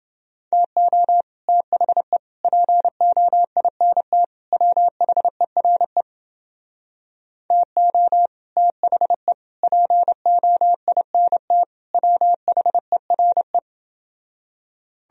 Morse code, heard as a keyed tone, about 30 wpm, 700 Hz. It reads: TO THE POINT WHERE TO THE POINT WHERE